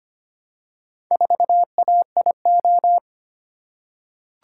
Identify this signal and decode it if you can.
Morse code, heard as a keyed tone, about 25 wpm, 700 Hz. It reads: 4AIO
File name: practice / 25wpm / Four Letter-Number Combinations - 25wpm